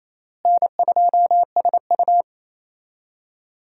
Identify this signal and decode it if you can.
Morse code, heard as a keyed tone, about 28 wpm, 700 Hz. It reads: N2SU
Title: USA Call Signs - 4 Characters - Repeated 3x - 28wpm